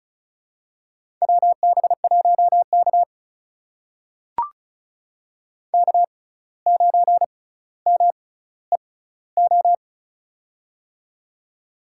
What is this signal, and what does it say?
Morse code, heard as a keyed tone, about 36 wpm, spaced as for 17 wpm, 700 Hz. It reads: WB1K K9MEO
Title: USA Call Signs - 6x Character Spacing - 35wpm